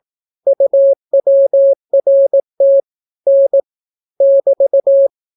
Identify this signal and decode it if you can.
Morse code, heard as a keyed tone, about 18 wpm, 550 Hz. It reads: UWRT N <BT>